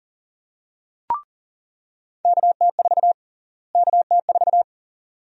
Morse code, heard as a keyed tone, about 40 wpm, 700 Hz. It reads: KT4 KT4
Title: USA Call Sign Prefixes - Repeated 2x - 40wpm